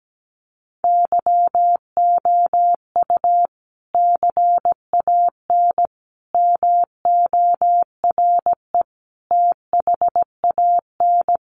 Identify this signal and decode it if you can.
Morse code, heard as a keyed tone, about 17 wpm, 700 Hz. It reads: YOU CAN MORE THAN
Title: Sentences from Top 1400 Words 17wpm